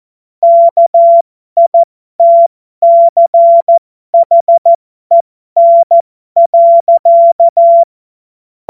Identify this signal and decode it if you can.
Morse code, heard as a keyed tone, about 14 wpm, spaced as for 12 wpm, 680 Hz. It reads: KITCHEN.
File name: Morsekurs / Witze